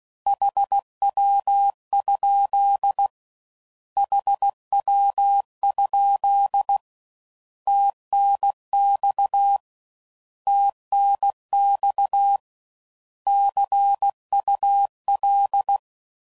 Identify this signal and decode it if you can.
Morse code, heard as a keyed tone, about 16 wpm, 800 Hz. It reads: HW? HW? TNX TNX CUL
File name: Mixed / Abreviations / Abreviations